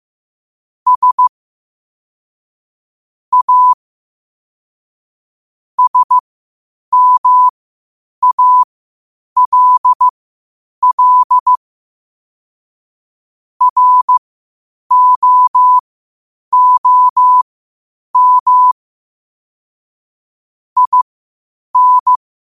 Morse code, heard as a keyed tone, about 15 wpm, spaced as for 8 wpm, 1000 Hz. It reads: S A SMALL ROOM IN
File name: qotdfiles / 8wpm